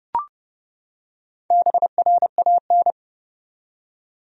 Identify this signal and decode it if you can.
Morse code, heard as a keyed tone, about 30 wpm, 700 Hz. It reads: BRAN
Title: USA Common Names - Max 4 Letters - 30wpm